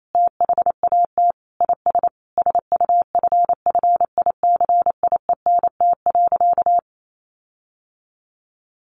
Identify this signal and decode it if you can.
Morse code, heard as a keyed tone, about 28 wpm, 700 Hz. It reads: THAT IS SUFFICIENT.
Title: Sentences from Top 1400 Words 28wpm